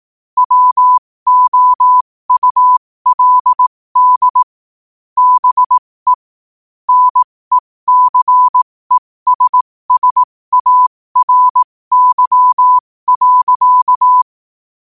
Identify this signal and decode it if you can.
Morse code, heard as a keyed tone, about 18 wpm, spaced as for 15 wpm, 1000 Hz. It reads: WOULD BE NECESSARY.
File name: qotdfiles / 15wpm